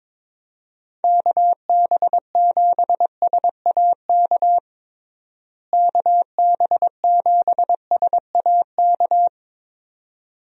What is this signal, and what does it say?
Morse code, heard as a keyed tone, about 22 wpm, 700 Hz. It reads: KB7SAK KB7SAK
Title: USA Call Signs - 6 Characters - Repeated 2x - 22wpm